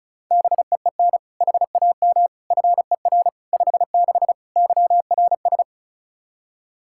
Morse code, heard as a keyed tone, about 34 wpm, 700 Hz. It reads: BEEN HAM FER 56 YRS
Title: Practice QSO - Exchange 3 - More Info - 35wpm